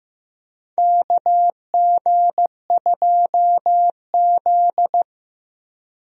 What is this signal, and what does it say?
Morse code, heard as a keyed tone, about 15 wpm, 700 Hz. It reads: KG2Z
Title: USA Call Signs - 4 Characters - 15wpm